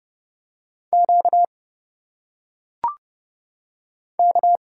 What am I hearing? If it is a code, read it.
Morse code, heard as a keyed tone, about 30 wpm, 700 Hz. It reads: Q K
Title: Single Letters - Phonetic Pronunciation 30wpm